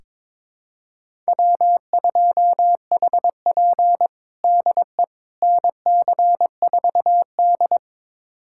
Morse code, heard as a keyed tone, about 22 wpm, 700 Hz. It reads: W2HP DE NC4D
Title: USA Callsign DE Callsign - 4 Characters - 22wpm